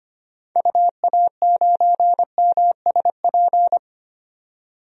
Morse code, heard as a keyed tone, about 25 wpm, 700 Hz. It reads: UA9MSP
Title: DXCC-100 Call Signs - 6 Characters - 25wpm